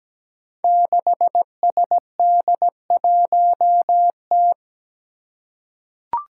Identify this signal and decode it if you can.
Morse code, heard as a keyed tone, about 17 wpm, 700 Hz. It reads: 6SD1T
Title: Five Challenging Letter-Number Combinations 17wpm